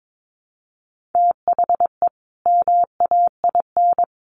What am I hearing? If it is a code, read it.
Morse code, heard as a keyed tone, about 22 wpm, 700 Hz. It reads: THE MAIN
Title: Additional Easy Phrases 22wpm